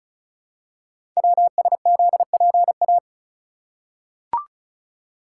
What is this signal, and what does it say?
Morse code, heard as a keyed tone, about 36 wpm, 700 Hz. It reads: WSZPA
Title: Code Groups - 5 Letters - 35wpm